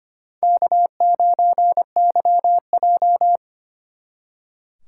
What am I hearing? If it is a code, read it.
Morse code, heard as a keyed tone, about 31 wpm, 700 Hz. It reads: K9YJ
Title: Speed Racing - USA 1x2 Call Signs 25wpm